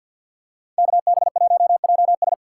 Morse code, heard as a keyed tone, about 47 wpm, 700 Hz. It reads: KB1JS
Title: USA Call Signs - 5 Characters - 4x Character Spacing - 50wpm